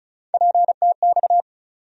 Morse code, heard as a keyed tone, about 34 wpm, 700 Hz. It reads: PTX
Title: USA Call Sign Postfixes - 3 Letters 35wpm